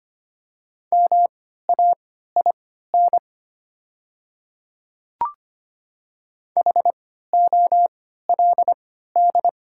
Morse code, heard as a keyed tone, about 25 wpm, spaced as for 14 wpm, 690 Hz. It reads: MAIN HOLD
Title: 3x Character Spacing - Top 500 Words - 25wpm